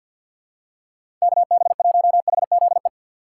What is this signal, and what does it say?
Morse code, heard as a keyed tone, about 47 wpm, 700 Hz. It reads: KB1HZE
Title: USA Call Signs - 6 Characters - 3x Character Spacing - 50wpm